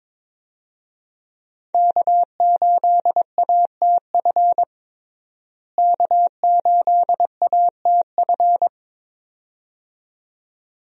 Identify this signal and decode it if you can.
Morse code, heard as a keyed tone, about 21 wpm, 700 Hz. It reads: K8ATF K8ATF
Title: USA Call Signs - 5 Characters - Repeated 2x - 22wpm